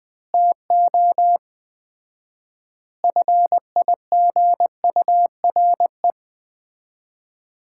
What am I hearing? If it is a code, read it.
Morse code, heard as a keyed tone, about 20 wpm, 700 Hz. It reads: TO FIGURE
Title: 4x Word-Spacing - Sets of 4 Words - 20wpm